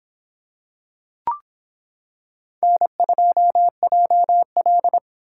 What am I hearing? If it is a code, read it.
Morse code, heard as a keyed tone, about 25 wpm, 700 Hz. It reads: N2JL